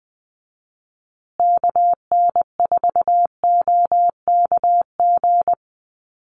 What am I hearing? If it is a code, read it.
Morse code, heard as a keyed tone, about 20 wpm, 700 Hz. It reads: KN4OKG
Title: USA Call Signs Repeated 3x 20wpm